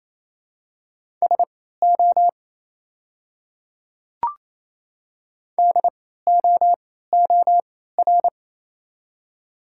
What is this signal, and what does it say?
Morse code, heard as a keyed tone, about 28 wpm, spaced as for 16 wpm, 700 Hz. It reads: SO DOOR
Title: 3x Character Spacing - Top 300 Words - 28wpm